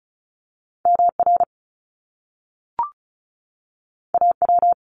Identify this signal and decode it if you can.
Morse code, heard as a keyed tone, about 36 wpm, 700 Hz. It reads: MR AW
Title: Top Two Letter Combinations in Words 35wpm